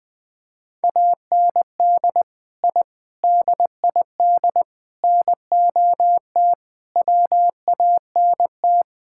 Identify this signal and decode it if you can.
Morse code, heard as a keyed tone, about 21 wpm, 700 Hz. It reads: AND I DID NOT WANT
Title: Sentences with 5-Word Phrases - 20wpm